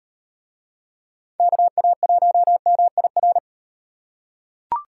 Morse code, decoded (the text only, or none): KA1MIR